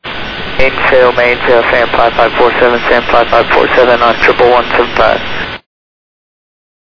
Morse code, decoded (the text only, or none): none